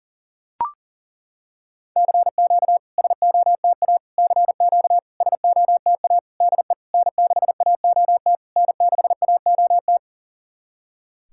Morse code, heard as a keyed tone, about 40 wpm, 690 Hz. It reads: CQ SOTA CQ SOTA DE N6AOT N6AOT
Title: SOTA QSO Practice 40wpm